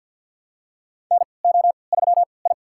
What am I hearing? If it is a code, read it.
Morse code, heard as a keyed tone, about 47 wpm, spaced as for 28 wpm, 700 Hz. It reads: NO3I
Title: USA Call Signs - 4 Characters - 3x Character Spacing - 50wpm